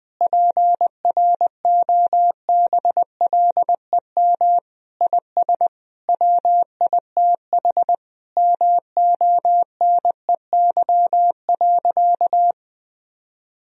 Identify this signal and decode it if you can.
Morse code, heard as a keyed tone, about 20 wpm, 700 Hz. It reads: PROBLEM IS WITH MONEY.
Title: Sentences from Top 1000 Words - Review Part 2 - 20wpm